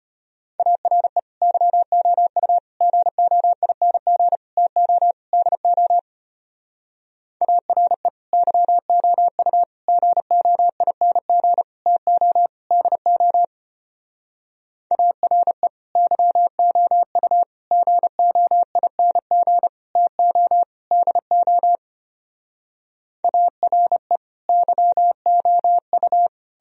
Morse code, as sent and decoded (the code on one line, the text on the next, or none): .- .-. / -.-- --- ..- / --. --- .. -. --. / - --- / -.. --- / .- .-. / -.-- --- ..- / --. --- .. -. --. / - --- / -.. --- / .- .-. / -.-- --- ..- / --. --- .. -. --. / - --- / -.. --- / .- .-. / -.-- --- ..-
ARE YOU GOING TO DO ARE YOU GOING TO DO ARE YOU GOING TO DO ARE YOU